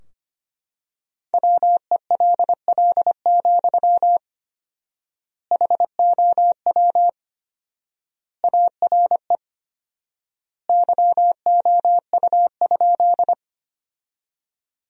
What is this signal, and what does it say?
Morse code, heard as a keyed tone, about 25 wpm, 700 Hz. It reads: WELL, HOW ARE YOU?
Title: 4x Word-Spacing - Sentences from Top 200 Words - 25wpm